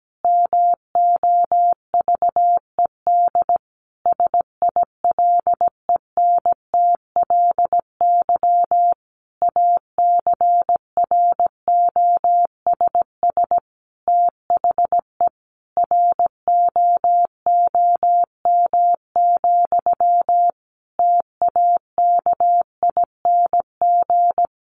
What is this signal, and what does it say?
Morse code, heard as a keyed tone, about 17 wpm, 700 Hz. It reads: MOVED SILENTLY ACROSS THE ROOM, TAKING
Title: A Sound Of Thunder 17wpm